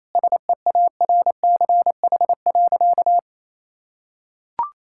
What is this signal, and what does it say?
Morse code, heard as a keyed tone, about 28 wpm, 700 Hz. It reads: SEARCH.
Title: Sentences from Top 1000 Words - Review Part 5 - 28wpm